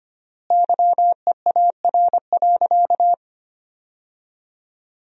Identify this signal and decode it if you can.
Morse code, heard as a keyed tone, about 25 wpm, 700 Hz. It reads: YEAR.
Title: 4x Word-Spacing - Sentences from Top 300 Words - 25wpm